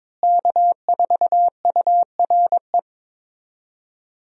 Morse code, heard as a keyed tone, about 22 wpm, 700 Hz. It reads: K4URE
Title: USA Call Signs - 5 Characters - Repeated 3x - 22wpm